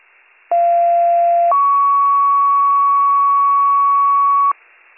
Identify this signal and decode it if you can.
Two-tone page, heard as a one-second tone, then a three-second tone, about 690 then 1120 Hz, with noise. It is heard from 0.5 to 4.5 s.